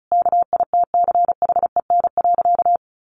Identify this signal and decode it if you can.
Morse code, heard as a keyed tone, about 34 wpm, 700 Hz. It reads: KITCHEN.
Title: Sentences from Top 1400 Words 35wpm